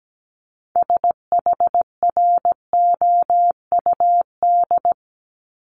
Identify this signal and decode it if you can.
Morse code, heard as a keyed tone, about 17 wpm, 700 Hz. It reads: SHROUD
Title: Multi-letter Phonemes 17wpm